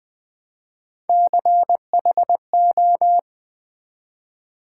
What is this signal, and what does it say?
Morse code, heard as a keyed tone, about 20 wpm, 700 Hz. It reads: CHO
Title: Build A Word - 1 Letter At A Time - 3 to 5-Letter Words - 20wpm